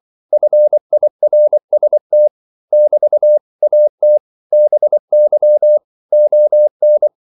Morse code, heard as a keyed tone, about 24 wpm, 600 Hz. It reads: FIRST <BT> AT BY ON
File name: CW / Practice / 100MostCommonEnglishWords24